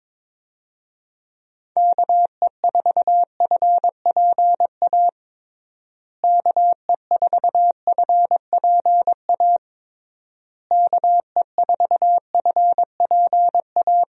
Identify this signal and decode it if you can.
Morse code, heard as a keyed tone, about 22 wpm, 700 Hz. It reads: KE4FPA KE4FPA KE4FPA
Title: USA Call Signs - 6 Characters - Repeated 3x - 22wpm